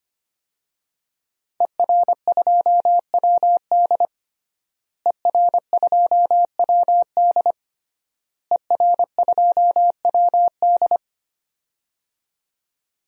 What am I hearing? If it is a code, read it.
Morse code, heard as a keyed tone, about 25 wpm, 700 Hz. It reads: ER2WD ER2WD ER2WD
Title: DXCC-100 Call Signs - 5 Characters - Repeated 3x - 25wpm